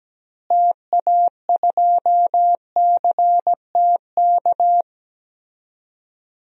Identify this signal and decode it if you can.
Morse code, heard as a keyed tone, about 17 wpm, 700 Hz. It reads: TA2CTK
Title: DXCC-100 Call Signs - 6 Characters - Repeated 3x - 17wpm